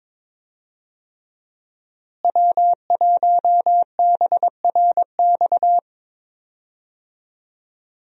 Morse code, heard as a keyed tone, about 22 wpm, 690 Hz. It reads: W1BRX
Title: USA Call Signs - 5 Characters - Repeated 3x - 22wpm